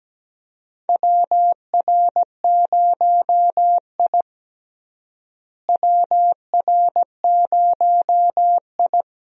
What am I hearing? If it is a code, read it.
Morse code, heard as a keyed tone, about 17 wpm, 700 Hz. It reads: WR0I WR0I
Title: USA Call Signs - 4 Characters - Repeated 3x - 17wpm